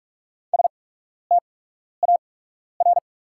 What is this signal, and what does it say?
Morse code, heard as a keyed tone, about 43 wpm, spaced as for 12 wpm, 690 Hz. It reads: STAR
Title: 8x Character Spacing - Top 500 Words - 45wpm